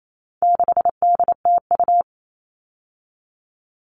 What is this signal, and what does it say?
Morse code, heard as a keyed tone, about 28 wpm, 700 Hz. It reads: 6DTU
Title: All Cut Numbers - 4-Digit 28wpm